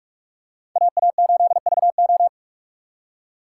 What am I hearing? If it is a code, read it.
Morse code, heard as a keyed tone, about 47 wpm, 700 Hz. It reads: AA8VO
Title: USA Call Signs - 5 Characters - 6x Character Spacing - 45wpm